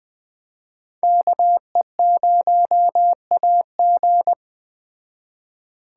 Morse code, heard as a keyed tone, about 20 wpm, 700 Hz. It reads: KE0AG
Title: Speed Racing - USA 2x2 Call Signs 17wpm